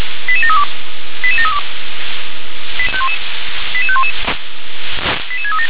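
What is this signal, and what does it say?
Five-tone selective call (ZVEI), heard as quick runs of five tones, tones of about 70 ms. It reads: 88732 88732 88511 88637
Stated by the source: ZVEI five-tone selective call received on an SDR.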